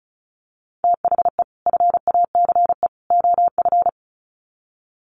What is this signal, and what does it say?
Morse code, heard as a keyed tone, about 34 wpm, 700 Hz. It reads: THE FACE OF
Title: Sets of 3 Words 35wpm